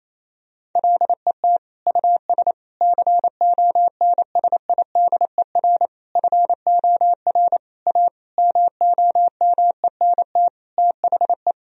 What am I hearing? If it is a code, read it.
Morse code, heard as a keyed tone, about 28 wpm, 700 Hz. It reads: LET US CONSIDER FOR A MOMENT THE
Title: Sentences from Top 1000 Words - Review Part 4 - 28wpm